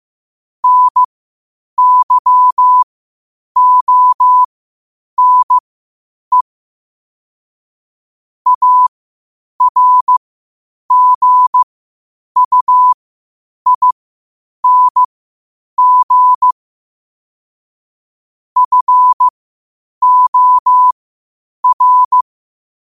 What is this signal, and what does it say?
Morse code, heard as a keyed tone, about 15 wpm, spaced as for 8 wpm, 1000 Hz. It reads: NYONE ARGUING FOR
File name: qotdfiles / 8wpm